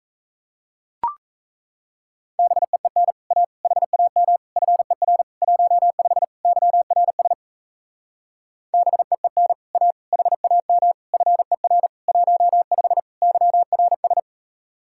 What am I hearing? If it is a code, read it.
Morse code, heard as a keyed tone, about 38 wpm, 700 Hz. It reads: BEEN A HAM FER 15 YRS BEEN A HAM FER 15 YRS